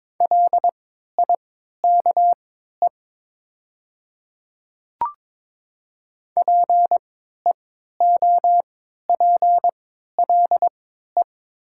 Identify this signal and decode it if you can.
Morse code, heard as a keyed tone, about 22 wpm, spaced as for 13 wpm, 700 Hz. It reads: LIKE PEOPLE
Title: 3x Character Spacing - Top 100 Words - 22wpm